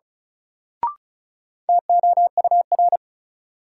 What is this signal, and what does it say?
Morse code, heard as a keyed tone, about 34 wpm, 700 Hz. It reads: TOUR
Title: Sentences from Top 1400 Words - New Word Drill Alt - 35wpm